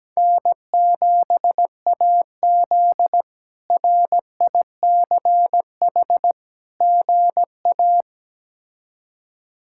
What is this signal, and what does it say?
Morse code, heard as a keyed tone, about 17 wpm, 700 Hz. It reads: N7AZ RICH GA
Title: K1USN SST QSO Practice - 17wpm